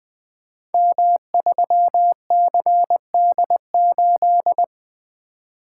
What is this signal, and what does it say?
Morse code, heard as a keyed tone, about 20 wpm, 700 Hz. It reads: M3CD8
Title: Code Groups - 5 Letter:Numbers - 20wpm